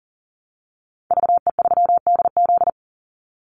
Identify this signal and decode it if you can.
Morse code, heard as a keyed tone, about 40 wpm, 700 Hz. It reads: VE3DZ
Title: Call Signs - 40wpm